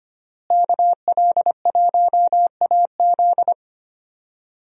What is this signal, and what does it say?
Morse code, heard as a keyed tone, about 25 wpm, 700 Hz. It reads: KL1AZ
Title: USA Call Signs - 5 Characters - Repeated 3x - 25wpm